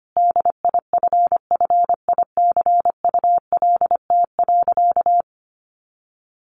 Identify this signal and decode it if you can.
Morse code, heard as a keyed tone, about 25 wpm, 700 Hz. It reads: DIFFICULT.
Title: Sentences from Top 1300 Words 25wpm